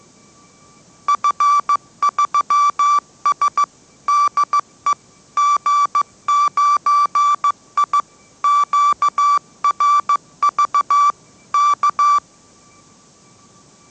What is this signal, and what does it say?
Morse code, heard as a keyed tone, about 15 wpm, 1160 Hz. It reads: F3S DE G9I QRV K
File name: mrx / mrx065 16wpm